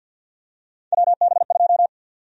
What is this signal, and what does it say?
Morse code, heard as a keyed tone, about 50 wpm, 700 Hz. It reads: WB2
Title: USA Call Sign Prefixes - Repeated 2x - 50wpm